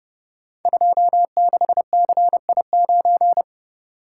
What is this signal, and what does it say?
Morse code, heard as a keyed tone, about 29 wpm, 700 Hz. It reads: 26CI9
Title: Code Groups - 5 Letter:Numbers - 30wpm